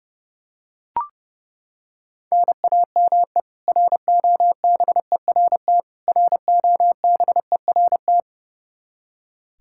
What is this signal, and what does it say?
Morse code, heard as a keyed tone, about 32 wpm, 700 Hz. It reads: NAME ROBERT ROBERT
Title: Practice QSO - Exchange 1 - Beginning - 30wpm